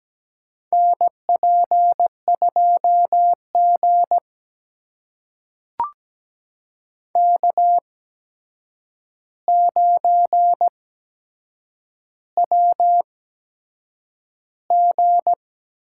Morse code, heard as a keyed tone, about 17 wpm, spaced as for 7 wpm, 700 Hz. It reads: NP2G K9WG
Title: USA Call Signs - 4 Characters - 8x Character Spacing - 17wpm